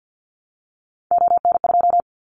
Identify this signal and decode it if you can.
Morse code, heard as a keyed tone, about 50 wpm, 700 Hz. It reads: ON2
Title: Callsign Prefixes for Top 50 DXCC - 50wpm